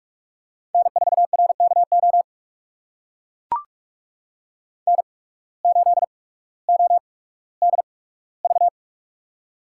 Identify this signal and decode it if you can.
Morse code, heard as a keyed tone, about 45 wpm, spaced as for 19 wpm, 700 Hz. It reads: N4RKO N8ODV
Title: USA Call Signs - 5 Characters - 8x Character Spacing - 45wpm